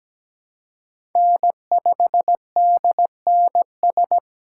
Morse code, heard as a keyed tone, about 17 wpm, 700 Hz. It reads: N5DNS
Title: USA Call Signs - 5 Characters - 2x Character Spacing - 17wpm